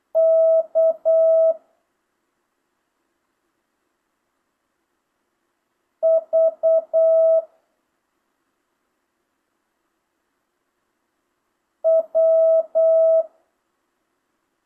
Morse code, heard as a keyed tone, about 8 wpm, 650 Hz. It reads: K V W
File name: jh8cbh / 2025cw